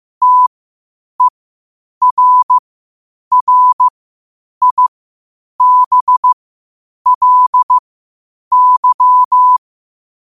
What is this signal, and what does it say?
Morse code, heard as a keyed tone, about 15 wpm, spaced as for 8 wpm, 1000 Hz. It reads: TERRIBLY
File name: qotdfiles / 8wpm